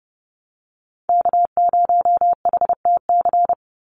Morse code, heard as a keyed tone, about 30 wpm, 700 Hz. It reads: K0HTC
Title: USA Call Signs 30wpm